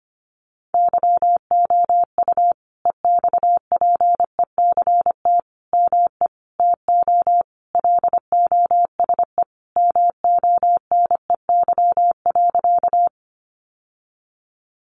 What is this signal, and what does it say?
Morse code, heard as a keyed tone, about 25 wpm, 700 Hz. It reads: YOU EXPECT ME TO LOSE MONEY.
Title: Sentences from Top 1100 Words 25wpm